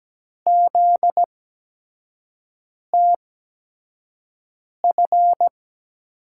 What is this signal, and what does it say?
Morse code, heard as a keyed tone, about 17 wpm, spaced as for 5 wpm, 700 Hz. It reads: ZTF
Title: USA Call Signs - 6 Characters - 8x Character Spacing - 17wpm